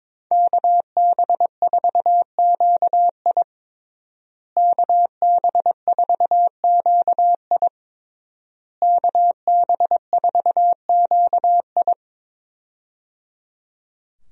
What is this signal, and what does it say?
Morse code, heard as a keyed tone, about 22 wpm, 700 Hz. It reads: KB4QI KB4QI KB4QI
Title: USA Call Signs - 5 Characters - Repeated 3x - 22wpm